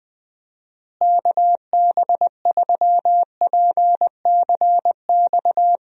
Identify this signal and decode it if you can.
Morse code, heard as a keyed tone, about 21 wpm, 700 Hz. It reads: KB3PCX
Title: USA Call Signs - 6 Characters - Repeated 3x - 20wpm